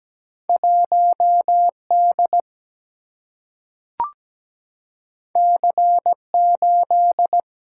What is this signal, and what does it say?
Morse code, heard as a keyed tone, about 17 wpm, 700 Hz. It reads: 1D C8
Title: Single-Letter Single-Number 17wpm